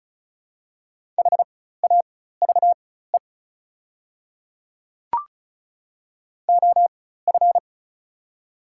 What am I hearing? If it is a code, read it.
Morse code, heard as a keyed tone, about 36 wpm, spaced as for 16 wpm, 700 Hz. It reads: HAVE OF
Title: 4x Character Spacing - Top 300 Words - 35wpm